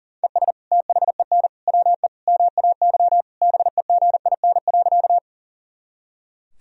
Morse code, heard as a keyed tone, about 40 wpm, 700 Hz. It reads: ES THEN WE MAY BEGIN.
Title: Sentences from Top 300 Words and Abbreviations 40wpm